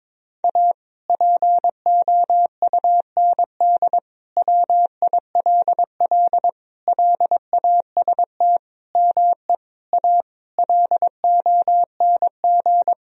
Morse code, heard as a keyed tone, about 22 wpm, 700 Hz. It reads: A POUND WILL LAST ME A LONG
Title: Sentences from Top 1000 Words - Review Part 2 - 22wpm